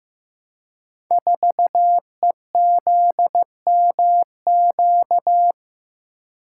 Morse code, heard as a keyed tone, about 15 wpm, 700 Hz. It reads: 4EZMQ
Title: Code Groups - 5 Letter:Numbers - 15wpm